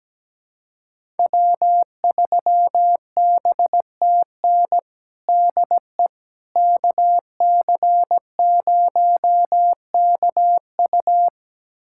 Morse code, heard as a keyed tone, about 17 wpm, 700 Hz. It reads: W3BTN DE KC0KU
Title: USA Callsign DE Callsign - 5 Characters - 17wpm